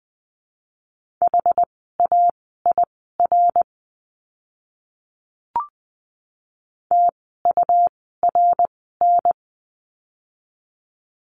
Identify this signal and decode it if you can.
Morse code, heard as a keyed tone, about 20 wpm, spaced as for 15 wpm, 700 Hz. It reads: HAIR TURN
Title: 101 Lesson 17f - Letter u - Words 20wpm